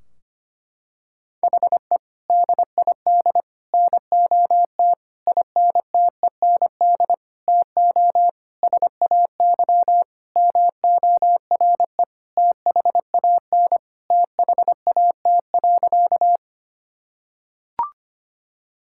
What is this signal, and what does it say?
Morse code, heard as a keyed tone, about 25 wpm, 700 Hz. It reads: HE DID NOT INTEND TO SAY MORE THAN THAT.
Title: Sentences from Top 1600 Words 25wpm